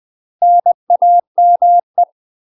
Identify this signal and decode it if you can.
Morse code, heard as a keyed tone, about 20 wpm, 700 Hz. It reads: NAME
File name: UK MASTER QSOs TRAINING / 20WPM / UK MASTER QSOs 0003